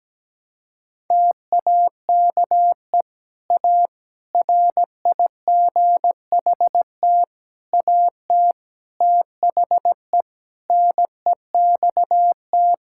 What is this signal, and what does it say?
Morse code, heard as a keyed tone, about 17 wpm, 700 Hz. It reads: TAKE A RIGHT AT THE NEXT
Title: Sentences from Top 1600 Words 17wpm